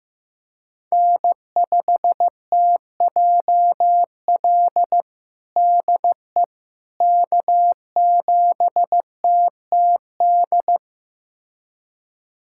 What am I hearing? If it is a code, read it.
Morse code, heard as a keyed tone, about 15 wpm, 700 Hz. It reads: N5TJL DE K7TTD
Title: USA Callsign DE Callsign - 5 Characters - 15wpm